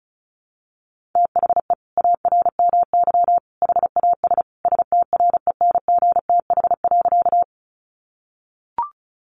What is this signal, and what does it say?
Morse code, heard as a keyed tone, about 34 wpm, 700 Hz. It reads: THE ARMY HAS STRENGTH.
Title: Sentences from Top 1300 Words 35wpm